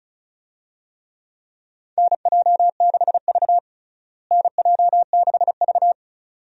Morse code, heard as a keyed tone, about 36 wpm, 700 Hz. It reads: NJ6V NJ6V
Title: USA Call Signs - 4 Characters - Repeated 2x - 35wpm